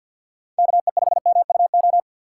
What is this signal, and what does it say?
Morse code, heard as a keyed tone, about 51 wpm, 700 Hz. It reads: KE5MUO
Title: USA Call Signs - 6x Character Spacing - 50wpm